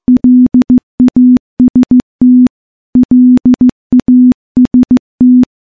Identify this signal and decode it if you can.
Morse code, heard as a keyed tone, about 15 wpm, 250 Hz. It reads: LAST LAST
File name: morse-training-15WPM